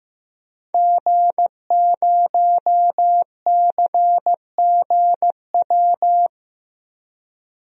Morse code, heard as a keyed tone, about 15 wpm, 700 Hz. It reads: G0CGW
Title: DXCC-100 Call Signs - 5 Characters - 15wpm